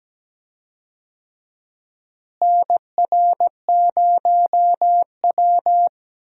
Morse code, heard as a keyed tone, about 17 wpm, 700 Hz. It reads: NR0W
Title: USA Call Signs - 4 Characters - Repeated 2x - 17wpm